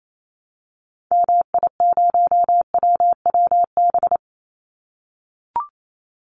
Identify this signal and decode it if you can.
Morse code, heard as a keyed tone, about 28 wpm, 700 Hz. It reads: MI0WWB
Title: Call Signs - 28wpm